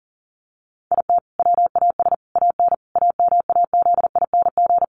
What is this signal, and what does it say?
Morse code, heard as a keyed tone, about 40 wpm, 700 Hz. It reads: IT WAS AN AMAZING
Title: Sentences from Top 1500 Words 40wpm